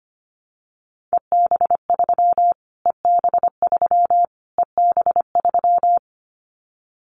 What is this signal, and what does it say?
Morse code, heard as a keyed tone, about 25 wpm, 700 Hz. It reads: EB3 EB3 EB3
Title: Callsign Prefixes for Top 25 DXCC - Repeated 3x - 25wpm